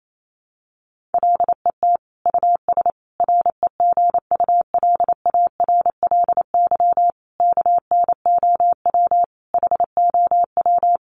Morse code, heard as a keyed tone, about 28 wpm, 700 Hz. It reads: LET US REGULARLY KNOW HOW
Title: Sentences from Top 1400 Words 28wpm